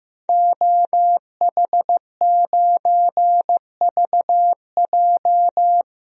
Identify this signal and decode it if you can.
Morse code, heard as a keyed tone, about 15 wpm, 700 Hz. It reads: OH9VJ
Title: DXCC-100 Call Signs - 5 Characters - 15wpm